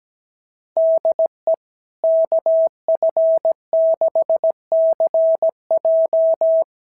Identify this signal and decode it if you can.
Morse code, heard as a keyed tone, about 17 wpm, 690 Hz. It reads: DE KF6CJ
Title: SOTA QSO Practice 17wpm